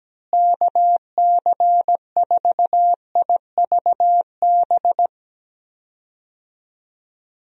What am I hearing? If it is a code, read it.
Morse code, heard as a keyed tone, about 17 wpm, 700 Hz. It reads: KC4IVB
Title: USA Call Signs - 6 Characters - Repeated 3x - 17wpm